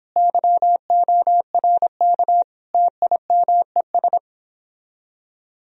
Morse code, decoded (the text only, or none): YORK TIMES